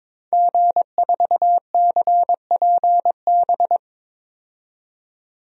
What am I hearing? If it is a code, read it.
Morse code, heard as a keyed tone, about 22 wpm, 700 Hz. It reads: G4CPB
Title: DXCC-100 Call Signs - 5 Characters - 22wpm